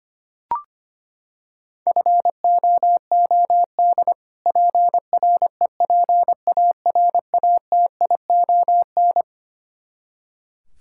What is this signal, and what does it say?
Morse code, heard as a keyed tone, about 25 wpm, 700 Hz. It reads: FOOD PREPARATION
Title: K1USN SST QSO Practice 25wpm